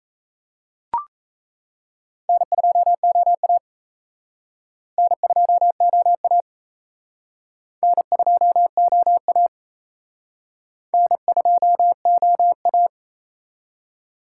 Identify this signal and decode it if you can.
Morse code, heard as a keyed tone, about 34 wpm, 700 Hz. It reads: N2OA N2OA N2OA N2OA
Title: Speed Racing - USA 1x2 Call Signs 28wpm